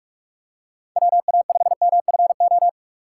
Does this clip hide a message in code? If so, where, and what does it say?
Yes: Morse code from 1.0 to 2.7 s, heard as a keyed tone, about 43 wpm, 700 Hz. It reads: WA5MFO